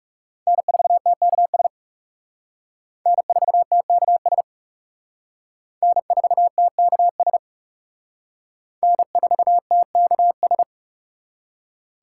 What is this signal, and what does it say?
Morse code, heard as a keyed tone, about 38 wpm, 700 Hz. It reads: N4TKS N4TKS N4TKS N4TKS